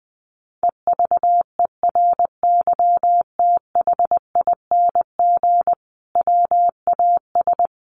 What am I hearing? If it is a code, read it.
Morse code, heard as a keyed tone, about 19 wpm, 700 Hz. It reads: EVERYTHING WAS
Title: Sentences from Top 1000 Words 20wpm